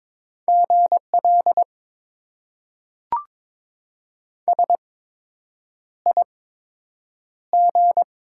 Morse code, heard as a keyed tone, about 22 wpm, spaced as for 8 wpm, 700 Hz. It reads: GL SIG
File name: Top 90 QSO Elements - 8x Character Spacing - 22wpm